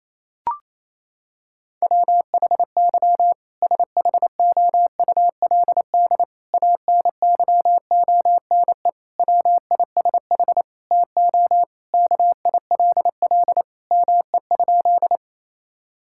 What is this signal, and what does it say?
Morse code, heard as a keyed tone, about 28 wpm, 700 Hz. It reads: WHY SHOULD ANYONE WISH TO KILL ME?
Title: Sentences from Top 1600 Words 28wpm